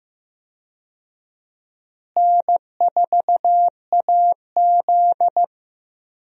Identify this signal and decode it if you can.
Morse code, heard as a keyed tone, about 15 wpm, 700 Hz. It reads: N4AZ
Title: USA Call Signs - 4 Characters - Repeated 2x - 15wpm